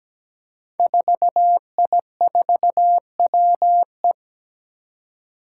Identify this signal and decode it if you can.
Morse code, heard as a keyed tone, about 17 wpm, 700 Hz. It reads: 4I4WE
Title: Five Challenging Letter-Number Combinations 17wpm